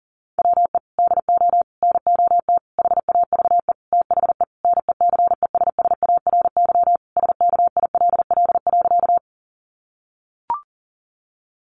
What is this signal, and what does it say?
Morse code, heard as a keyed tone, about 40 wpm, 700 Hz. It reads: WE DO NOT HAVE THE NECESSARY SKILL.
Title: Sentences from Top 1200 Words 40wpm